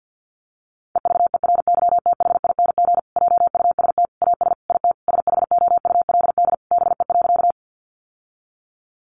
Morse code, heard as a keyed tone, about 50 wpm, 700 Hz. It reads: EVERYTHING JUST AS IT SHOULD BE.
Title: Sentences from Top 700 Words 50wpm